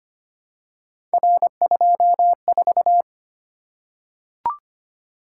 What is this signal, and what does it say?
Morse code, heard as a keyed tone, about 25 wpm, 700 Hz. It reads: R24
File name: Three Letter-Number Combinations - 25wpm